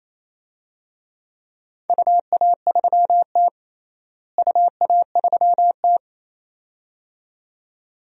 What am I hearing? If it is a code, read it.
Morse code, heard as a keyed tone, about 28 wpm, 700 Hz. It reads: UA3T UA3T
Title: DXCC-100 Call Signs - 4 Characters - Repeated 2x - 28wpm